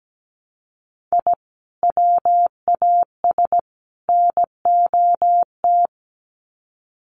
Morse code, heard as a keyed tone, about 17 wpm, 700 Hz. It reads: I WAS NOT
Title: Sets of 3 Words 17wpm